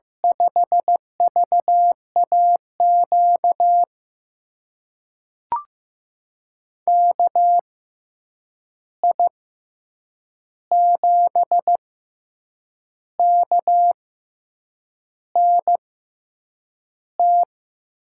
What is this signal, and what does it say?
Morse code, heard as a keyed tone, about 15 wpm, spaced as for 7 wpm, 700 Hz. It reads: 5VAQ KI7KNT
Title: USA Call Signs - 6 Characters - 6x Character Spacing - 15wpm